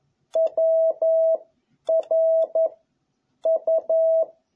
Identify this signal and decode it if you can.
Morse code, heard as a keyed tone, about 11 wpm, 660 Hz. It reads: W R U
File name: jh8cbh / 2021cw